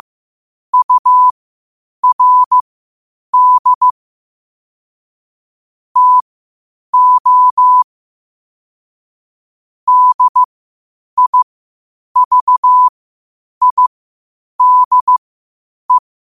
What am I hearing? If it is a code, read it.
Morse code, heard as a keyed tone, about 15 wpm, spaced as for 8 wpm, 1000 Hz. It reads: URD TO DIVIDE
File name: qotdfiles / 8wpm